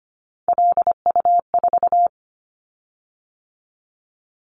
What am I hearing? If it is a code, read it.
Morse code, heard as a keyed tone, about 25 wpm, 700 Hz. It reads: LU4
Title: Callsign Prefixes for Top 50 DXCC - 25wpm